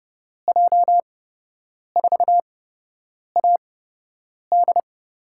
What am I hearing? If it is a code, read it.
Morse code, heard as a keyed tone, about 30 wpm, spaced as for 8 wpm, 700 Hz. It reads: J4AD
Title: USA Call Signs - 6 Characters - 8x Character Spacing - 30wpm